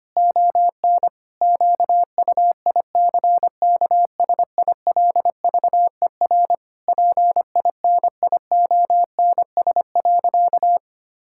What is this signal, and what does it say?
Morse code, heard as a keyed tone, about 25 wpm, 700 Hz. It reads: ON QUICKSILVER PINIONS.
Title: Pangram Challenge v2 - 25wpm